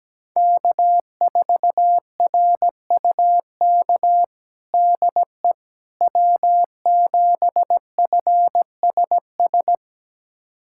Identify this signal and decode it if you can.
Morse code, heard as a keyed tone, about 17 wpm, 700 Hz. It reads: K4RUK DE W7FSS
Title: USA Callsign DE Callsign - 5 Characters - 17wpm